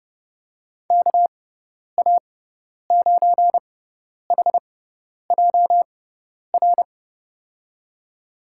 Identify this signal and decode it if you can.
Morse code, heard as a keyed tone, about 30 wpm, spaced as for 10 wpm, 700 Hz. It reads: KA9HJR
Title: USA Call Signs - 6 Characters - 6x Character Spacing - 30wpm